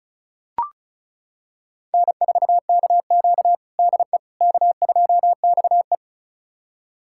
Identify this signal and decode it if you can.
Morse code, heard as a keyed tone, about 36 wpm, 700 Hz. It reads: N4KQ DE K2XE
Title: USA Callsign DE Callsign - 4 Characters - 35wpm